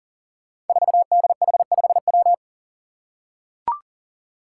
Morse code, heard as a keyed tone, about 40 wpm, 700 Hz. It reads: 4DH5W
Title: Five Challenging Letter-Number Combinations 40wpm